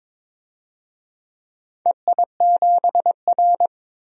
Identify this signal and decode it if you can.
Morse code, heard as a keyed tone, about 22 wpm, 700 Hz. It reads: EI7R
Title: DXCC-100 Call Signs - 4 Characters - 22wpm